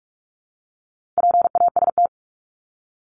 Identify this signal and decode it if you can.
Morse code, heard as a keyed tone, about 44 wpm, 700 Hz. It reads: PAST
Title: Top 501-750 Words 45wpm